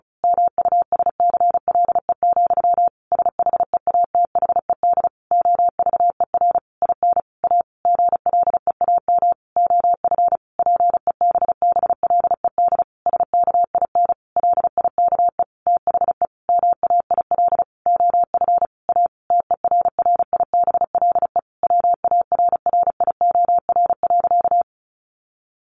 Morse code, heard as a keyed tone, about 34 wpm, 700 Hz. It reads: MUSCLE, SHEATHED OVER IN A GLEAM OF PEBBLED SKIN LIKE THE MAIL OF A TERRIBLE WARRIOR.
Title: A Sound Of Thunder 35wpm